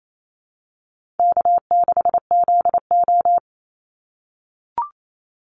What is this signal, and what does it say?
Morse code, heard as a keyed tone, about 28 wpm, 700 Hz. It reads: K6ZO
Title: Call Signs - 28wpm